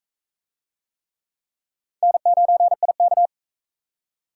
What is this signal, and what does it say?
Morse code, heard as a keyed tone, about 34 wpm, 700 Hz. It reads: N9IK
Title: Speed Racing - USA Call Signs 28wpm